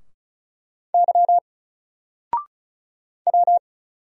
Morse code, heard as a keyed tone, about 36 wpm, 700 Hz. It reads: Y W
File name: Single-Letter-Number-35wpm